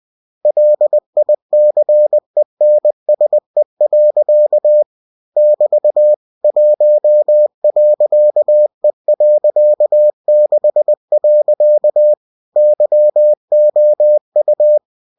Morse code, heard as a keyed tone, about 20 wpm, 600 Hz. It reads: LICENSE. <BT> 1.E.6. YOU
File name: cw-books / winnie-the-pooh / winnie-20wpm-page-0116